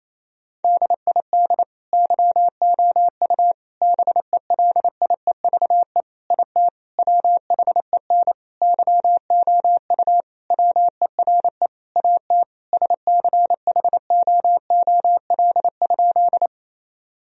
Morse code, heard as a keyed tone, about 28 wpm, 700 Hz. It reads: DID YOU BELIEVE IT WHEN YOU WERE AT SCHOOL?
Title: Sentences from Top 1000 Words - Review Part 1 - 28wpm